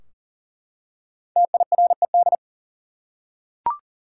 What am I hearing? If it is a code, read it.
Morse code, heard as a keyed tone, about 40 wpm, 700 Hz. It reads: TIRED
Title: Five Letter Words - 40wpm